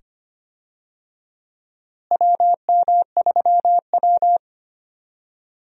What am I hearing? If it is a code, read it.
Morse code, heard as a keyed tone, about 25 wpm, 700 Hz. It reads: WM3W
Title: USA Call Signs - 4 Characters - Repeated 2x - 25wpm